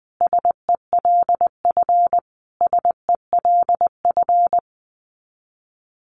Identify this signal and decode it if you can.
Morse code, heard as a keyed tone, about 19 wpm, 700 Hz. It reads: SELF SELF
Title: Top 500 Words Repeated 20wpm